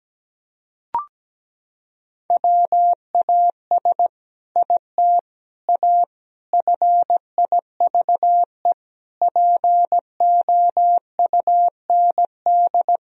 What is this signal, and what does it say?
Morse code, heard as a keyed tone, about 17 wpm, 700 Hz. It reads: WAS IT A FIVE POUND
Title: Sentences from Top 1000 Words - Review Part 2 - 17wpm